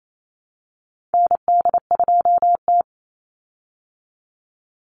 Morse code, heard as a keyed tone, about 28 wpm, 700 Hz. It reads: ND2T
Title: Call Signs Repeated 2x - 28wpm